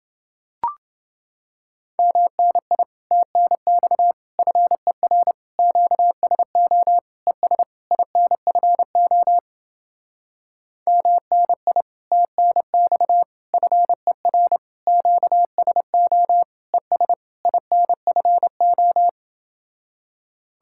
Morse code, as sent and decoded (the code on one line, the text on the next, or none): -- -. .. / - -. -..- / ..-. .-. / --.- ... --- / . ... / .. -. ..-. --- / -- -. .. / - -. -..- / ..-. .-. / --.- ... --- / . ... / .. -. ..-. ---
MNI TNX FER QSO ES INFO MNI TNX FER QSO ES INFO